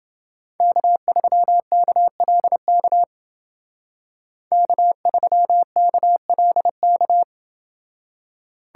Morse code, heard as a keyed tone, about 29 wpm, 700 Hz. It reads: K3KLK K3KLK